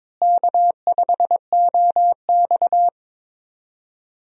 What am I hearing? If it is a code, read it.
Morse code, heard as a keyed tone, about 22 wpm, 700 Hz. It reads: K5OX
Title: USA Call Signs - 4 Characters - Repeated 3x - 22wpm